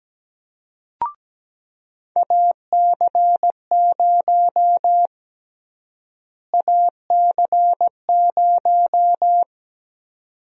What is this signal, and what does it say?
Morse code, heard as a keyed tone, about 17 wpm, 700 Hz. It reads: AC0 AC0
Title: USA Call Sign Prefixes - Repeated 2x - 17wpm